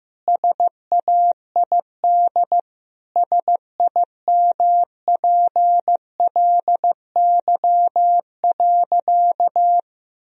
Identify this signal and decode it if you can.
Morse code, heard as a keyed tone, about 16 wpm, 700 Hz. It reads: SAID SIMPLY.
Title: Sentences from Top 1000 Words - Review Part 4 - 15wpm